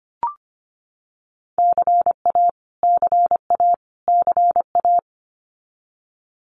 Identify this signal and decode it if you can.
Morse code, heard as a keyed tone, about 25 wpm, 700 Hz. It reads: CA CA CA
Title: US States Repeated 3x 25wpm